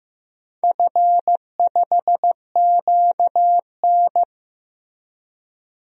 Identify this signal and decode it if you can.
Morse code, heard as a keyed tone, about 15 wpm, 700 Hz. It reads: F5QN
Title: DXCC-100 Call Signs - 4 Characters - Repeated 3x - 15wpm